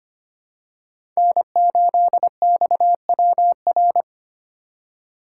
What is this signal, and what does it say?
Morse code, heard as a keyed tone, about 25 wpm, 700 Hz. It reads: N8XWR
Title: USA Call Signs - 4x Character Spacing - 25wpm